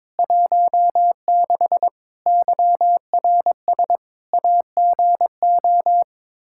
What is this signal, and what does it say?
Morse code, heard as a keyed tone, about 22 wpm, 700 Hz. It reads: 16 YRS AGO
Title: Practice QSO - Exchange 3 - More Info - 22wpm